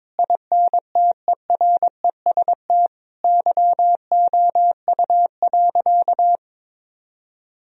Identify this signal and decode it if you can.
Morse code, heard as a keyed tone, about 22 wpm, 700 Hz. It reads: INTEREST YOU.
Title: Sentences from Top 1000 Words - Review Part 4 - 22wpm